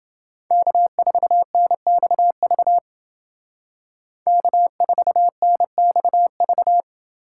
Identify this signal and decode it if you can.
Morse code, heard as a keyed tone, about 28 wpm, 700 Hz. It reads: K4NXV K4NXV